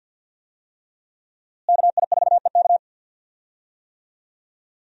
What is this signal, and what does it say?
Morse code, heard as a keyed tone, about 52 wpm, 700 Hz. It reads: KI4EK
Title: USA Call Signs - 5 Characters - Repeated 3x - 50wpm